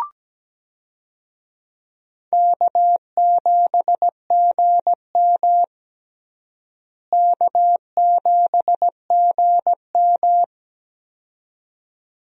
Morse code, heard as a keyed tone, about 17 wpm, 700 Hz. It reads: K7GM K7GM
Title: USA Call Signs - 4 Characters - Repeated 2x - 17wpm